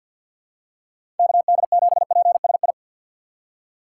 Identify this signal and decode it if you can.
Morse code, heard as a keyed tone, about 50 wpm, 700 Hz. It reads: KD7PSI
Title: USA Call Signs - 4x Character Spacing - 50wpm